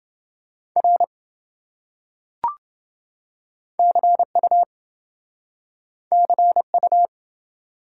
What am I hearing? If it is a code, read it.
Morse code, heard as a keyed tone, about 27 wpm, 700 Hz. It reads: R CU CU